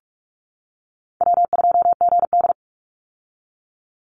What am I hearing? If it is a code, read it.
Morse code, heard as a keyed tone, about 44 wpm, 700 Hz. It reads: W2GD
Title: Easy Call Signs - 45wpm